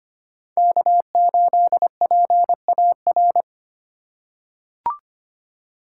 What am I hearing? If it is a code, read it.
Morse code, heard as a keyed tone, about 25 wpm, 700 Hz. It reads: K8PAR
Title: USA Call Signs - 5 Characters - 3x Character Spacing - 25wpm